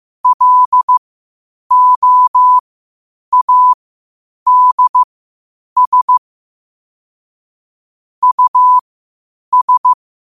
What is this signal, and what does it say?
Morse code, heard as a keyed tone, about 15 wpm, spaced as for 8 wpm, 1000 Hz. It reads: LOADS US
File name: qotdfiles / 8wpm